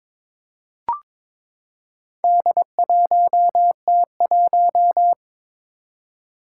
Morse code, heard as a keyed tone, about 22 wpm, 700 Hz. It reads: D1T1
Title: Four Challenging Letter-Number Combinations 22wpm